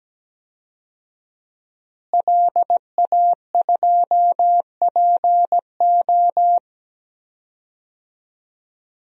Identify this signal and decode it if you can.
Morse code, heard as a keyed tone, about 17 wpm, 700 Hz. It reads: LA2PO
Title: DXCC-100 Call Signs - 5 Characters - Repeated 3x - 17wpm